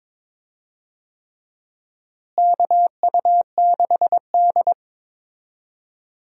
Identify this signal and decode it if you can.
Morse code, heard as a keyed tone, about 22 wpm, 700 Hz. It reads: KU6D
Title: USA Call Signs - 4 Characters - Repeated 3x - 22wpm